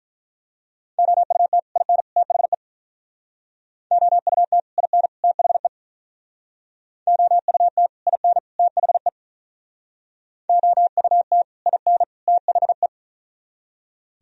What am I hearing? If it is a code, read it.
Morse code, heard as a keyed tone, about 47 wpm, 700 Hz. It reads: OUT IN THE OUT IN THE OUT IN THE OUT IN THE